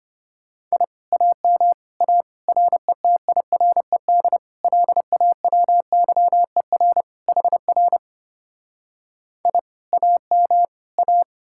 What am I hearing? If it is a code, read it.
Morse code, heard as a keyed tone, about 33 wpm, 700 Hz. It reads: I AM A RETIRED LAWYER HR I AM A